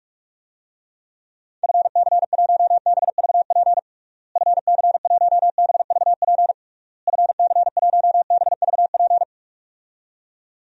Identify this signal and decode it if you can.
Morse code, heard as a keyed tone, about 42 wpm, 700 Hz. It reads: FC1BVP FC1BVP FC1BVP
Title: DXCC-100 Call Signs - 6 Characters - Repeated 3x - 45wpm